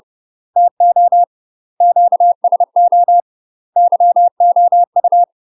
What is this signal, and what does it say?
Morse code, heard as a keyed tone, about 30 wpm, 700 Hz. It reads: TO QSO YOU
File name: UK MASTER QSOs TRAINING / 30WPM / UK MASTER QSOs 0029